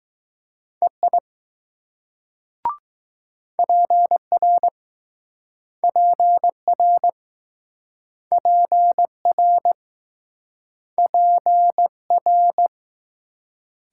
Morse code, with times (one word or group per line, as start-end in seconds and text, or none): 0.8-1.2 EI
3.6-4.7 PR
5.8-7.1 PR
8.3-9.7 PR
11.0-11.9 P
12.1-12.7 R